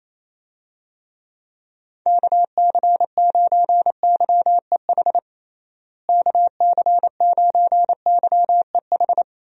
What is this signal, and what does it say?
Morse code, heard as a keyed tone, about 27 wpm, 700 Hz. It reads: KC9YEH KC9YEH
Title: USA Call Signs - 6 Characters - Repeated 2x - 28wpm